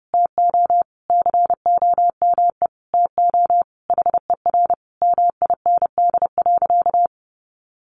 Morse code, heard as a keyed tone, about 28 wpm, 700 Hz. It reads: TO COME TO HER MIND.
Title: Sentences from Top 1300 Words 30wpm